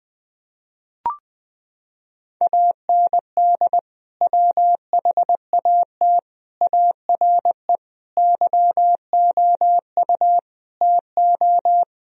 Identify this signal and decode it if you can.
Morse code, heard as a keyed tone, about 20 wpm, 700 Hz. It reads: AND WHAT ARE YOU TO
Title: Sentences from Top 1000 Words - Review Part 2 - 20wpm